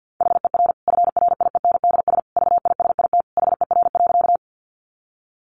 Morse code, heard as a keyed tone, about 50 wpm, 700 Hz. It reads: HER FRIENDS VISIT HER.
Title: Easy Phrases 50wpm